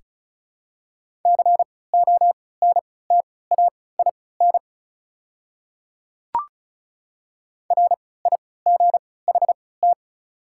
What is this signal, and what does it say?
Morse code, heard as a keyed tone, about 36 wpm, spaced as for 20 wpm, 700 Hz. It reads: CONTAIN RIGHT
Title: 3x Character Spacing - Top 500 Words - 35wpm